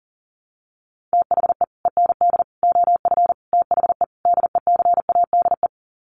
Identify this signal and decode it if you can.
Morse code, heard as a keyed tone, about 40 wpm, 700 Hz. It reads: THE END OF THE DECADE
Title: Sets of 5 Words Encore 40wpm